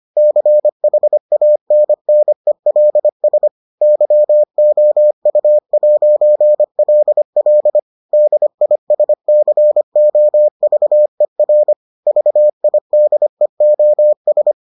Morse code, heard as a keyed tone, about 25 wpm, 600 Hz. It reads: CHANNELS YOU'LL DISCOVER VIDEOS